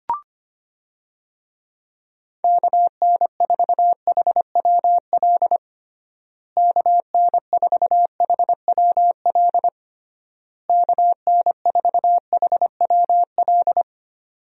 Morse code, heard as a keyed tone, about 25 wpm, 700 Hz. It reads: KN4HWL KN4HWL KN4HWL
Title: USA Call Signs - 6 Characters - Repeated 3x - 25wpm